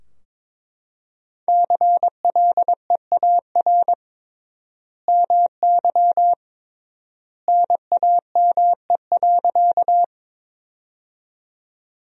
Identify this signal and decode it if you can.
Morse code, heard as a keyed tone, about 22 wpm, 700 Hz. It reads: CLEAR MY NAME.
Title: 3x Word-Spacing - Sentences from Top 500 Words - 22wpm